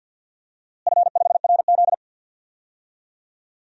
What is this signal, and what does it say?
Morse code, heard as a keyed tone, about 52 wpm, 700 Hz. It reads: F5RZ
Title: DXCC-100 Call Signs - 4 Characters - Repeated 3x - 50wpm